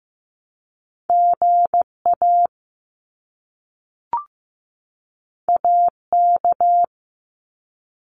Morse code, heard as a keyed tone, about 15 wpm, 700 Hz. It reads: GA AK
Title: US States 15wpm